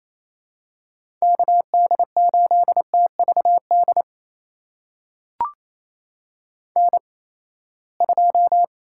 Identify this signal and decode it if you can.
Morse code, heard as a keyed tone, about 27 wpm, spaced as for 19 wpm, 700 Hz. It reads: KD8TVD N2
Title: USA Call Signs - 8x Character Spacing - 28wpm